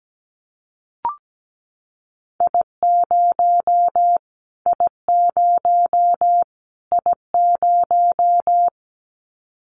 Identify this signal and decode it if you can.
Morse code, heard as a keyed tone, about 17 wpm, 700 Hz. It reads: I0 I0 I0
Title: Callsign Prefixes for Top 25 DXCC - Repeated 3x - 17wpm